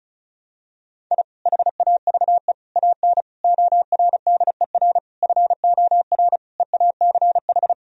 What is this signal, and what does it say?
Morse code, heard as a keyed tone, about 36 wpm, 700 Hz. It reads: I HAVE AN ORDER FOR EACH
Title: Sentences from Top 1000 Words - Review Part 2 - 35wpm